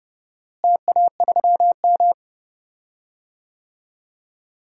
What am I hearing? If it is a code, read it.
Morse code, heard as a keyed tone, about 32 wpm, 700 Hz. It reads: TA3M
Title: DXCC-100 Call Signs - 4 Characters - Repeated 3x - 30wpm